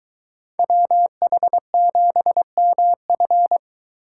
Morse code, heard as a keyed tone, about 23 wpm, 700 Hz. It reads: WH7MF